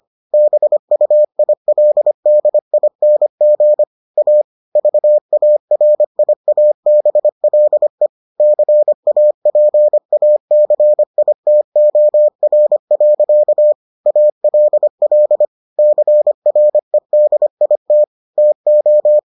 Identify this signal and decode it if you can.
Morse code, heard as a keyed tone, about 25 wpm, 600 Hz. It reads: BUILDING A VARIABLE CAPACITOR. ALL CREDIT TO